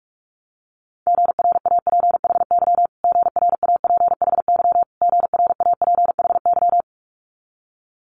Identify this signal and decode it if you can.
Morse code, heard as a keyed tone, about 46 wpm, 700 Hz. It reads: GRAPHY GRAPHY GRAPHY
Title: Multi-letter Phonemes Repeated 3x 45wpm